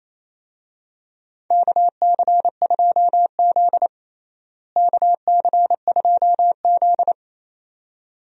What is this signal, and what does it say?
Morse code, heard as a keyed tone, about 28 wpm, 700 Hz. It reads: KC2Z KC2Z
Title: USA Call Signs - 4 Characters - Repeated 2x - 28wpm